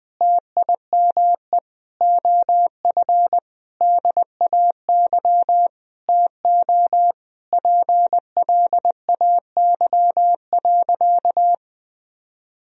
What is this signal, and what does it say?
Morse code, heard as a keyed tone, about 21 wpm, 700 Hz. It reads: TIME OF DAY TO PLAY.
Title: Sentences from Top 1000 Words - Review Part 1 - 20wpm